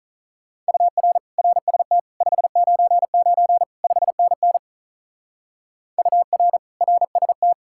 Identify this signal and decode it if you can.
Morse code, heard as a keyed tone, about 46 wpm, 700 Hz. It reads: UR RST 599 5NN UR RST